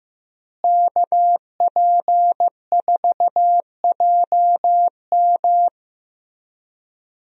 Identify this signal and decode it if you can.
Morse code, heard as a keyed tone, about 15 wpm, 700 Hz. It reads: KP4JM
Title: DXCC-100 Call Signs - 5 Characters - Repeated 3x - 15wpm